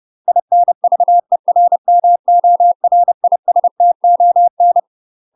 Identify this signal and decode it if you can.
Morse code, heard as a keyed tone, about 30 wpm, 700 Hz. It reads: INVERMORISTON
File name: UK MASTER QSOs TRAINING / 30WPM / UK MASTER QSOs 0004